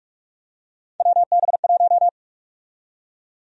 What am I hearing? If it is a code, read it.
Morse code, heard as a keyed tone, about 46 wpm, 700 Hz. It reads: WB1
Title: USA Call Sign Prefixes - 45wpm